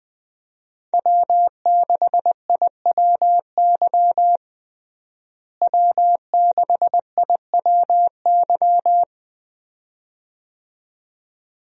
Morse code, heard as a keyed tone, about 20 wpm, 700 Hz. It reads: W6IWY W6IWY
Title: USA Call Signs - 5 Characters - Repeated 2x - 20wpm